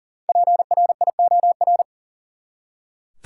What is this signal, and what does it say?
Morse code, heard as a keyed tone, about 37 wpm, 700 Hz. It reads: PRIOR
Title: Five Letter Words - 40wpm